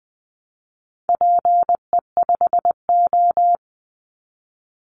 Morse code, heard as a keyed tone, about 20 wpm, 700 Hz. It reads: PE5O
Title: Call Signs Repeated 3x - 20wpm